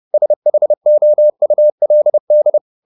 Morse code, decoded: SHOULD

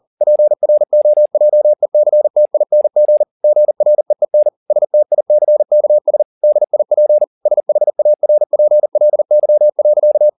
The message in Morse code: PROJECTING GREEN STICKS DIP SHARPLY.

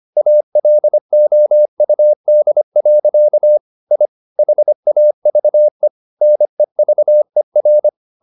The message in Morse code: ALOUD. I HAVE NEVER

